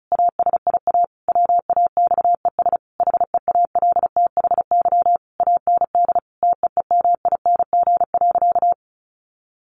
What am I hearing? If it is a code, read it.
Morse code, heard as a keyed tone, about 34 wpm, 700 Hz. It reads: ASIA WAXES HEALTHY AND TEEMING.
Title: A Sound Of Thunder 35wpm